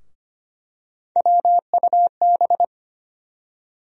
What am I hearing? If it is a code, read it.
Morse code, heard as a keyed tone, about 25 wpm, 700 Hz. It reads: WUB
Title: USA Call Sign Postfixes - 3 Letters 25wpm